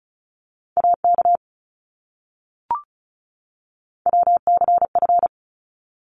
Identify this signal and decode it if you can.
Morse code, heard as a keyed tone, about 34 wpm, 700 Hz. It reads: AK WCF
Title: ARRL Sections 35wpm